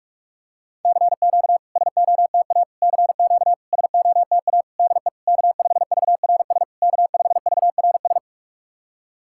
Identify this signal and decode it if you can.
Morse code, heard as a keyed tone, about 44 wpm, 700 Hz. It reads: CQ SOTA CQ SOTA DE K5VRS K5VRS
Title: SOTA QSO Practice 45wpm